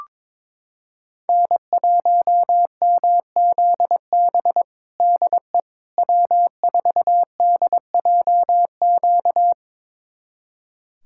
Morse code, heard as a keyed tone, about 21 wpm, 700 Hz. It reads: N1MZB DE W4DJQ
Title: USA Callsign DE Callsign - 5 Characters - 22wpm